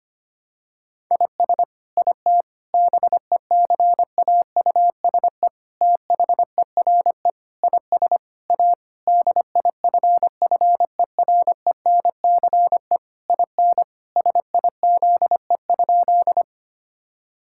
Morse code, heard as a keyed tone, about 25 wpm, 700 Hz. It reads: IS IT BECAUSE THERE IS A DIFFERENCE IN SIZE?
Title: Sentences from Top 1000 Words - Review Part 5 - 25wpm